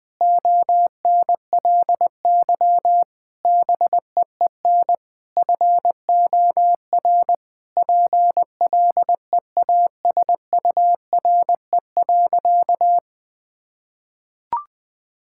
Morse code, heard as a keyed tone, about 20 wpm, 700 Hz. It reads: ONLY BEEN FOR PLEASURE.
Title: Sentences from Top 1600 Words 20wpm